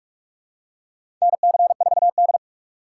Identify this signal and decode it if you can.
Morse code, heard as a keyed tone, about 44 wpm, 700 Hz. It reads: NC4D
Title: USA Call Signs - 4 Characters - 6x Character Spacing - 45wpm